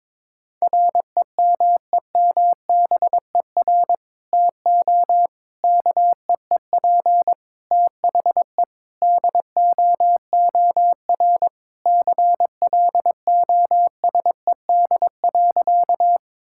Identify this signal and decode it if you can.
Morse code, heard as a keyed tone, about 22 wpm, 700 Hz. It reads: REMEMBER TO KEEP THE DOOR CLOSED.
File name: practice / 22wpm / Sentences with 3-Word Phrases - 22wpm